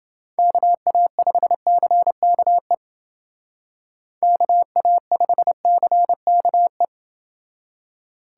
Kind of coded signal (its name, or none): Morse code